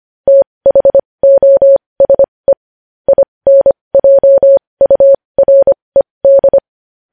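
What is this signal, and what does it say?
Morse code, heard as a keyed tone, about 25 wpm, spaced as for 20 wpm, 550 Hz. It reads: THOSE INJURED
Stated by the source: Morse code news headlines sent at 20.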